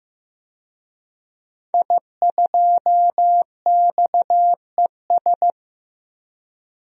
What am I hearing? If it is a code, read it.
Morse code, heard as a keyed tone, about 15 wpm, 700 Hz. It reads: I2XES
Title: DXCC-100 Call Signs - 5 Characters - 15wpm